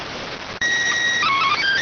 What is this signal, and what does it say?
SSTV (slow-scan television), PD 290.